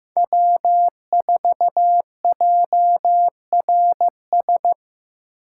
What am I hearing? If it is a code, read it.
Morse code, heard as a keyed tone, about 18 wpm, 700 Hz. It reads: W4JRS
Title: Speed Racing - USA 1x3 Call Signs 15wpm